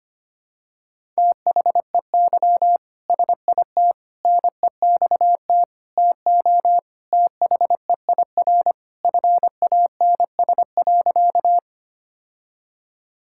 Morse code, heard as a keyed tone, about 25 wpm, 700 Hz. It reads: THEY SIT NEXT TO THEIR FANS.
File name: practice / 25wpm / All-Age Appropriate Jokes - 25wpm